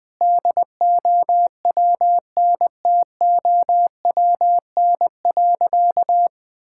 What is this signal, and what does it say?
Morse code, heard as a keyed tone, about 21 wpm, 700 Hz. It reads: DOWNTOWN.
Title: Sentences with 6-Word Phrases - Spoken Beginning - 20wpm